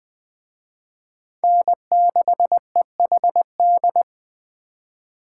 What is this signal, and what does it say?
Morse code, heard as a keyed tone, about 20 wpm, 700 Hz. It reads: N6EHD
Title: Speed Racing - USA Call Signs 17wpm